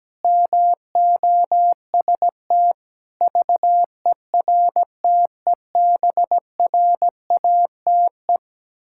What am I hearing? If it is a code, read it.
Morse code, heard as a keyed tone, about 17 wpm, 700 Hz. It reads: MOST VERTEBRATE
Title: Pangram Challenge v2 - 17wpm